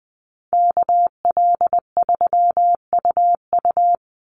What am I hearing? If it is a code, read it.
Morse code, heard as a keyed tone, about 20 wpm, 700 Hz. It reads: KL3UU
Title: USA Call Signs 20wpm